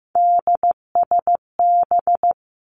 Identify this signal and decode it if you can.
Morse code, heard as a keyed tone, about 15 wpm, 700 Hz. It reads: DSB
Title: Three Challenging Letter-Number Combinations 15wpm